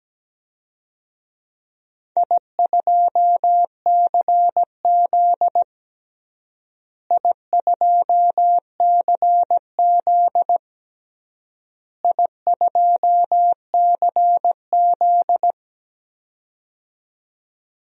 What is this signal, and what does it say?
Morse code, heard as a keyed tone, about 17 wpm, 700 Hz. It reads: I2CZ I2CZ I2CZ
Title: DXCC-100 Call Signs - 4 Characters - Repeated 3x - 17wpm